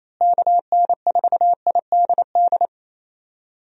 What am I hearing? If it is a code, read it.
Morse code, heard as a keyed tone, about 28 wpm, 700 Hz. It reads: KN4IDD
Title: USA Call Signs - 6 Characters - Repeated 3x - 28wpm